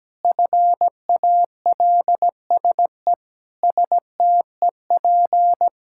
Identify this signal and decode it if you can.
Morse code, heard as a keyed tone, about 17 wpm, 700 Hz. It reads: FALSE STEP
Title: Sentences from Top 1600 Words 17wpm